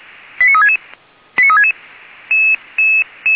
Five-tone selective call (ZVEI), heard as quick runs of five tones, tones of about 70 ms. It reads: 89366 89366
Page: ZVEI